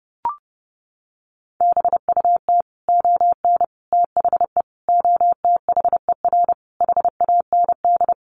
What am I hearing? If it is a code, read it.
Morse code, heard as a keyed tone, about 28 wpm, 700 Hz. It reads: BUT ON THE OTHER HAND
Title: Sets of 5 Words 30wpm